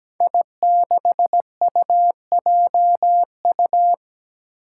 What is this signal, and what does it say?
Morse code, heard as a keyed tone, about 17 wpm, 700 Hz. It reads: I6UJU
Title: Five Challenging Letter-Number Combinations 17wpm